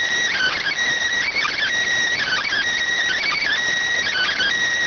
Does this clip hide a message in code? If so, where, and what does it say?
Yes: Emergency Alert System (SAME) tones (header bursts) from 0.3 to 4.5 s, heard as harsh buzzing bursts.